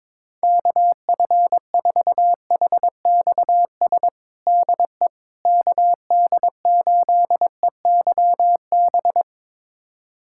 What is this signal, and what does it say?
Morse code, heard as a keyed tone, about 22 wpm, 700 Hz. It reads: KF4HXS DE KD8EYB
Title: USA Callsign DE Callsign - 6 Characters - 22wpm